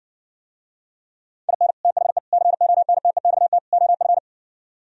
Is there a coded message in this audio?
Yes: Morse code from 1.5 to 4.2 s, heard as a keyed tone, about 60 wpm, 700 Hz. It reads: IN THE CONTEXT OF